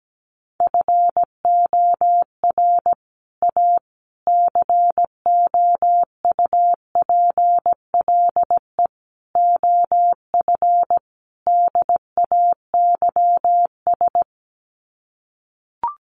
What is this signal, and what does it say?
Morse code, heard as a keyed tone, about 17 wpm, 700 Hz. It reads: FOR A COUPLE OF DAYS
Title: Sets of 5 Words Encore2 17wpm